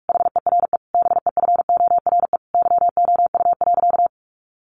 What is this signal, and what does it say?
Morse code, heard as a keyed tone, about 44 wpm, 700 Hz. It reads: HERE BEFORE YOU.
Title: Sentences from Top 1400 Words 45wpm